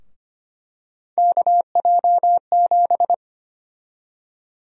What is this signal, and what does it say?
Morse code, heard as a keyed tone, about 25 wpm, 700 Hz. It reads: KJ7
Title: USA Call Sign Prefixes - Repeated 2x - 25wpm